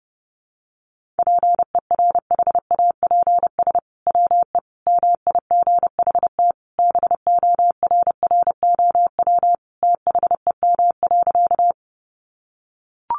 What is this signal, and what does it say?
Morse code, heard as a keyed tone, about 30 wpm, 700 Hz. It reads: PERHAPS WE MIGHT BORROW THEM.
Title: Sentences from Top 1500 Words 30wpm